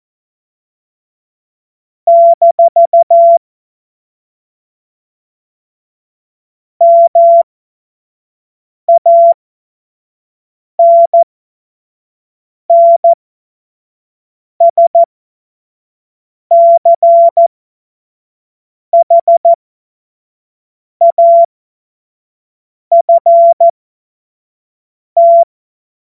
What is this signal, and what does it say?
Morse code, heard as a keyed tone, about 14 wpm, spaced as for 5 wpm, 680 Hz. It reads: - MANNSCHAFT
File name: Morsekurs / Witze / Witz 27. April 2021, 25 F70